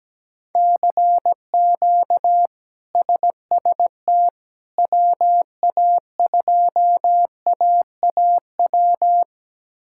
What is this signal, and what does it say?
Morse code, heard as a keyed tone, about 17 wpm, 700 Hz. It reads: CQ SST WA2AAW
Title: K1USN SST QSO Practice - 17wpm